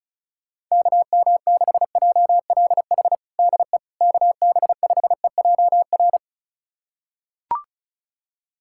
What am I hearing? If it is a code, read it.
Morse code, heard as a keyed tone, about 34 wpm, 700 Hz. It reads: KM6JLH DE KB5EJR
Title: USA Callsign DE Callsign - 6 Characters - 35wpm